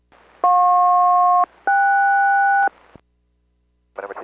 Aircraft SELCAL, heard as two pairs of tones. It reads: JP-KS